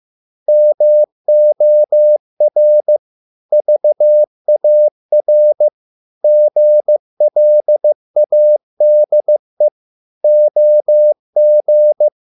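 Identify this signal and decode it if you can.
Morse code, heard as a keyed tone, about 15 wpm, 600 Hz. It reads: MOR VAR GLADE OG
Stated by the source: Morse code at a slow 15 wpm from CW med Gnister.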